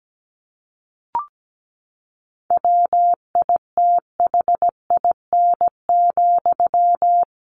Morse code, heard as a keyed tone, about 17 wpm, 700 Hz. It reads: WITHIN,
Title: A Sound Of Thunder 17wpm